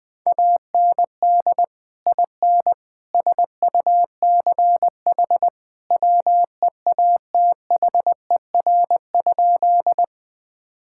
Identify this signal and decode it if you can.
Morse code, heard as a keyed tone, about 20 wpm, 700 Hz. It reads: AND IN SUCH WEATHER?
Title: Sentences from Top 1000 Words - Review Part 5 - 20wpm